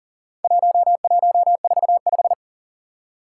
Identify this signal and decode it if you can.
Morse code, heard as a keyed tone, about 40 wpm, 700 Hz. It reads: 1145
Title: Four Challenging Letter-Number Combinations 40wpm